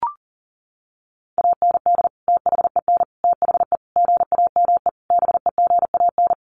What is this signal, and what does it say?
Morse code, heard as a keyed tone, about 40 wpm, 700 Hz. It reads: AND THEN THE GAME BEGAN
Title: Sentences from Top 500 Words 40wpm